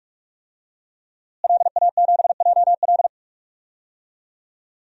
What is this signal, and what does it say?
Morse code, heard as a keyed tone, about 41 wpm, 700 Hz. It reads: LA7JL
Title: DXCC-100 Call Signs - 5 Characters - Repeated 3x - 45wpm